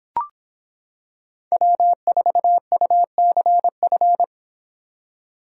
Morse code, heard as a keyed tone, about 26 wpm, 700 Hz. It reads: W4UCF